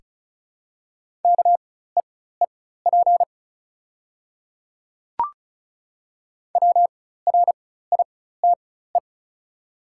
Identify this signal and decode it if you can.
Morse code, heard as a keyed tone, about 36 wpm, spaced as for 16 wpm, 690 Hz. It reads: KEEP WRITE
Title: 4x Character Spacing - Top 500 Words - 35wpm